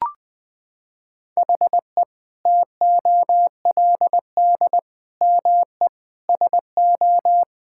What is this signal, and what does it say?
Morse code, heard as a keyed tone, about 20 wpm, 700 Hz. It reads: HE TOLD ME SO
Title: Sentences from Top 1600 Words 20wpm